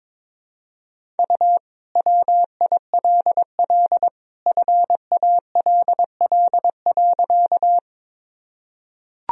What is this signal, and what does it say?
Morse code, heard as a keyed tone, about 22 wpm, 700 Hz. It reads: U WILL FALL.
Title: Sentences from Top 500 Words and Abbreviations 22wpm